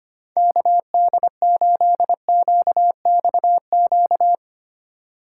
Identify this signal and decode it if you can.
Morse code, heard as a keyed tone, about 25 wpm, 700 Hz. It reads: KD8QXQ
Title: USA Call Signs - 6 Characters - 4x Character Spacing - 25wpm